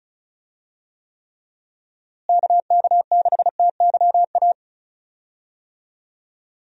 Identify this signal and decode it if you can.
Morse code, heard as a keyed tone, about 36 wpm, 700 Hz. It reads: KK6TYA
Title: USA Call Signs - 6 Characters - Repeated 2x - 35wpm